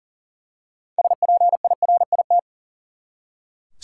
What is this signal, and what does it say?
Morse code, heard as a keyed tone, about 40 wpm, 700 Hz. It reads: SPIRIT
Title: Six Letter Words - 40wpm